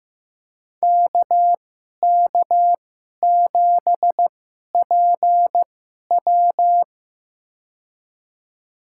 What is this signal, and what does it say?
Morse code, heard as a keyed tone, about 15 wpm, spaced as for 11 wpm, 700 Hz. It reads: KK7PW
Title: USA Call Signs - 5 Characters - 2x Character Spacing - 15wpm